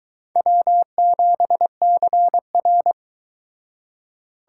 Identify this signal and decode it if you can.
Morse code, heard as a keyed tone, about 22 wpm, 700 Hz. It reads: W7CR